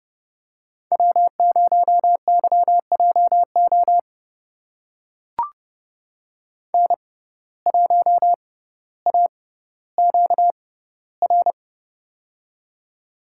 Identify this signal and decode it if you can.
Morse code, heard as a keyed tone, about 30 wpm, spaced as for 15 wpm, 700 Hz. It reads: W0YJO N1AQR
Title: USA Call Signs - 5 Characters - 6x Character Spacing - 30wpm